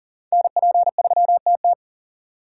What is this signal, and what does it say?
Morse code, heard as a keyed tone, about 40 wpm, 690 Hz. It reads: NP3TT
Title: USA Call Signs - 5 Characters - 3x Character Spacing - 40wpm